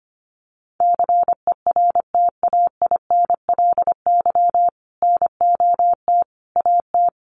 Morse code, heard as a keyed tone, about 25 wpm, 700 Hz. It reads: CERTAINLY NOT AT
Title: Sentences from Top 900 Words 25wpm